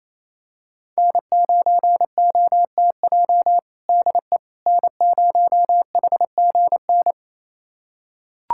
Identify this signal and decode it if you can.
Morse code, heard as a keyed tone, about 28 wpm, 700 Hz. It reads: N9OTJ DE N0HGN
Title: USA Callsign DE Callsign - 5 Characters - 28wpm